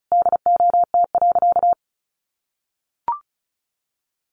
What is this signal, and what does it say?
Morse code, heard as a keyed tone, about 34 wpm, 700 Hz. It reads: DOT.
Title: Sentences from Top 1500 Words 35wpm